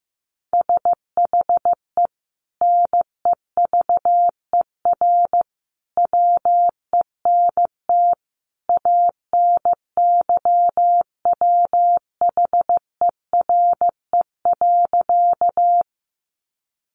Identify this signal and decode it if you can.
Morse code, heard as a keyed tone, about 15 wpm, 700 Hz. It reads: SHE NEVER WENT ANYWHERE.
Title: Sentences from Top 1400 Words 15wpm